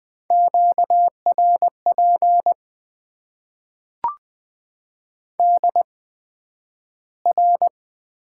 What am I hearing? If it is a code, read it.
Morse code, heard as a keyed tone, about 20 wpm, spaced as for 11 wpm, 700 Hz. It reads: QRP DR
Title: Top 90 QSO Elements - 8x Character Spacing - 20wpm